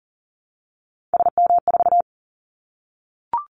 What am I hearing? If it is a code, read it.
Morse code, heard as a keyed tone, about 40 wpm, 700 Hz. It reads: SM4
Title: Callsign Prefixes for Top 25 DXCC - 40wpm